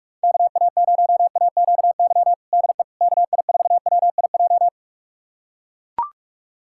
Morse code, heard as a keyed tone, about 41 wpm, 690 Hz. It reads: KA0AQY DE KI4WIJ
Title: USA Callsign DE Callsign - 6 Characters - 45wpm